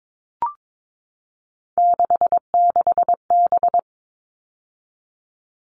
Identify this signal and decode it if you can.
Morse code, heard as a keyed tone, about 22 wpm, 700 Hz. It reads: 66B
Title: All Cut Numbers - 3-Digit 22wpm